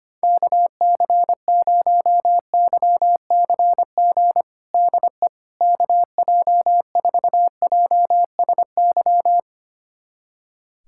Morse code, heard as a keyed tone, about 25 wpm, 700 Hz. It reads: KC0YCG DE KJ4JSY
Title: USA Callsign DE Callsign - 6 Characters - 25wpm